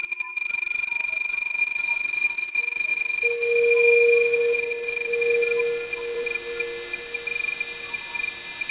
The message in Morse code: none